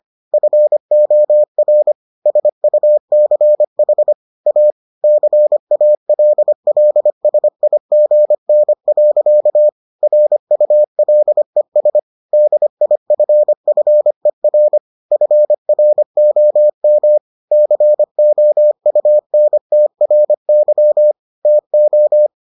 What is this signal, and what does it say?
Morse code, heard as a keyed tone, about 25 wpm, 600 Hz. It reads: FOR SUCH A CALLSIGN. RULES DIFFER FROM COUNTRY TO